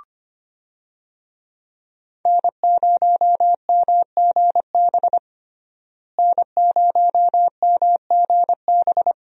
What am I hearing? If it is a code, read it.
Morse code, heard as a keyed tone, about 25 wpm, 700 Hz. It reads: N0MGB N0MGB
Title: USA Call Signs - 5 Characters - Repeated 3x - 25wpm